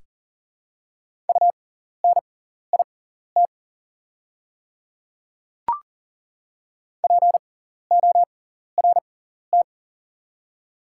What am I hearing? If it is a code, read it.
Morse code, heard as a keyed tone, about 40 wpm, spaced as for 14 wpm, 700 Hz. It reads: UNIT PORT
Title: 6x Character Spacing - Top 500 Words - 40wpm